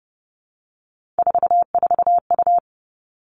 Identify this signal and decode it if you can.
Morse code, heard as a keyed tone, about 30 wpm, 700 Hz. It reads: 44U
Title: Three Challenging Letter-Number Combinations 30wpm